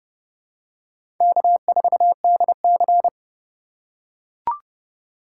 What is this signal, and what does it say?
Morse code, heard as a keyed tone, about 30 wpm, 700 Hz. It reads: K4DC
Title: USA Call Signs - 2x Character Spacing - 30wpm